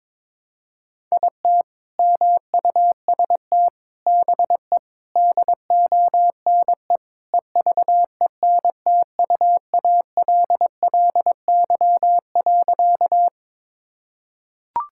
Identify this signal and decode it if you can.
Morse code, heard as a keyed tone, about 22 wpm, 700 Hz. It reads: IT MUST BE DONE EVENTUALLY.
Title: Sentences from Top 1000 Words - Review Part 6 - 22wpm